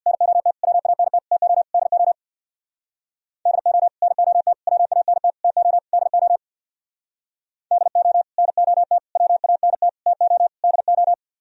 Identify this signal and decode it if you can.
Morse code, heard as a keyed tone, about 60 wpm, 700 Hz. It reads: NOT WANT TO DO DO NOT WANT TO DO DO NOT WANT TO DO
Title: Speed-Racing - Sets of 5 Words 50wpm